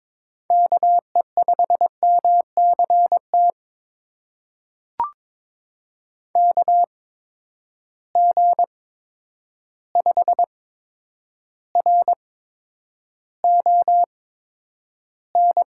Morse code, heard as a keyed tone, about 22 wpm, spaced as for 9 wpm, 700 Hz. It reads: KE5MCT KG5RON
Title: USA Call Signs - 6 Characters - 8x Character Spacing - 22wpm